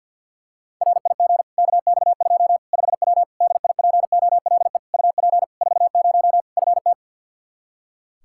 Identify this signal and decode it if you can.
Morse code, heard as a keyed tone, about 52 wpm, 700 Hz. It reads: RIG KX2 5W DIPOLE UP 40 FT
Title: Practice QSO - Exchange 2 - Info - 50wpm